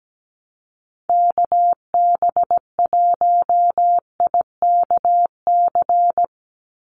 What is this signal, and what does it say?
Morse code, heard as a keyed tone, about 17 wpm, 700 Hz. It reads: KB1IKC
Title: USA Call Signs 17wpm